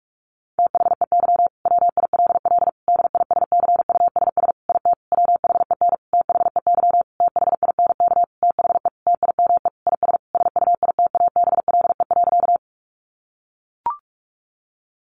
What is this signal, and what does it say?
Morse code, heard as a keyed tone, about 44 wpm, 700 Hz. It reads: THEY WILL DISCUSS IT WHEN THEY THINK THE TIME IS SUITABLE.
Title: Sentences from Top 1400 Words 45wpm